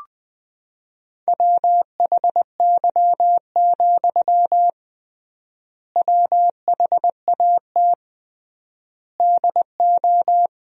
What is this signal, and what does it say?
Morse code, heard as a keyed tone, about 21 wpm, 700 Hz. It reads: WHY, WHAT DO
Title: 3x Word-Spacing - Sentences from Top 200 Words - 20wpm